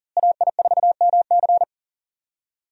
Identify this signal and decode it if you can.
Morse code, heard as a keyed tone, about 40 wpm, 700 Hz. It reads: AI4MC
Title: USA Call Signs - 5 Characters - Repeated 3x - 40wpm